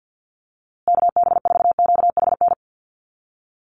Morse code, heard as a keyed tone, about 50 wpm, 700 Hz. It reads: KB4QHN
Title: USA Call Signs Repeated 3x 50wpm